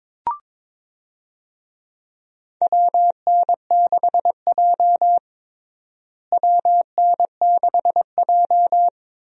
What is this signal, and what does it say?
Morse code, heard as a keyed tone, about 22 wpm, 700 Hz. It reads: WN6J WN6J
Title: USA Call Signs - 4 Characters - Repeated 2x - 22wpm